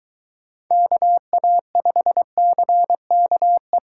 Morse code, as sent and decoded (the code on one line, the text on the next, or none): -.- .- ..... -.-. -.- .
KA5CKE